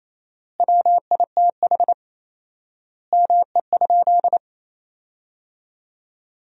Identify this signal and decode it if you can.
Morse code, heard as a keyed tone, about 27 wpm, 700 Hz. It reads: WITH ME?
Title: 4x Word-Spacing - Sentences from Top 300 Words - 28wpm